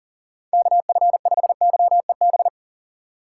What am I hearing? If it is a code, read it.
Morse code, heard as a keyed tone, about 40 wpm, 700 Hz. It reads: KF5YEB
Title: USA Call Signs - 6 Characters - 3x Character Spacing - 40wpm